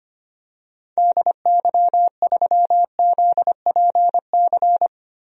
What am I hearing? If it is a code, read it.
Morse code, heard as a keyed tone, about 25 wpm, 700 Hz. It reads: DY3ZPC
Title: DXCC-100 Call Signs - 6 Characters - Repeated 3x - 25wpm